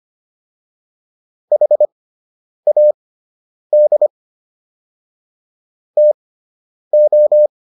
Morse code, heard as a keyed tone, about 25 wpm, spaced as for 9 wpm, 600 Hz. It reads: HAD TO